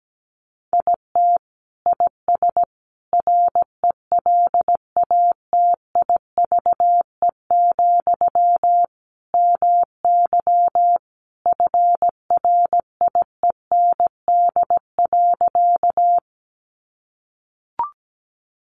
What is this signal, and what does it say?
Morse code, heard as a keyed tone, about 17 wpm, 700 Hz. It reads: IT IS RELATIVE, MY FRIEND.
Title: Sentences from Top 1400 Words 17wpm